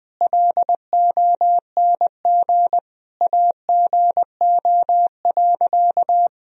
Morse code, heard as a keyed tone, about 20 wpm, 700 Hz. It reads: LONG AGO.
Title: Sentences from Top 1000 Words - Review Part 3 - 20wpm